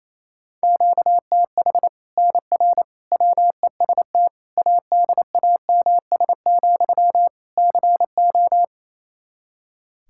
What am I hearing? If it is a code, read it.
Morse code, heard as a keyed tone, about 28 wpm, 700 Hz. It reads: QTH NR WEST ADAMS, CO
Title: Practice QSO - Exchange 1 - Beginning - 28wpm